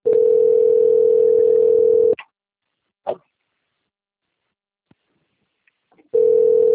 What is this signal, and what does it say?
Ringback tone (North America), 440 + 480 Hz.